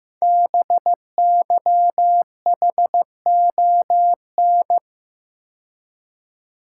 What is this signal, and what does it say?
Morse code, heard as a keyed tone, about 15 wpm, 700 Hz. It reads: BYHON
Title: Code Groups - 5 Letters - 15wpm